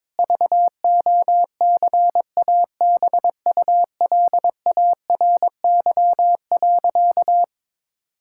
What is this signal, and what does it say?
Morse code, heard as a keyed tone, about 22 wpm, 700 Hz. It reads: VOCABULARY.
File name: Difficult Sentences with Hints - 22wpm